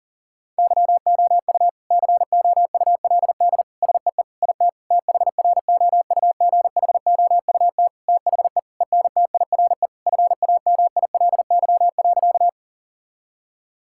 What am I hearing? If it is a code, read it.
Morse code, heard as a keyed tone, about 40 wpm, 700 Hz. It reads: YOU COULD SEE IT THROUGHOUT THE ENTIRE FAMILY.
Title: Sentences from Top 1000 Words - Review Part 5 - 40wpm